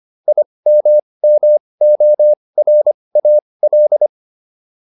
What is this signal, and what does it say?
Morse code, heard as a keyed tone, about 25 wpm, spaced as for 20 wpm, 600 Hz. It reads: IMMORAL